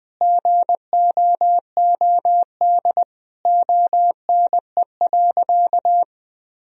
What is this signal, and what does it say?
Morse code, heard as a keyed tone, about 20 wpm, 700 Hz. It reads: GOOD ONE.
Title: Sentences from Top 1000 Words - Review Part 3 - 20wpm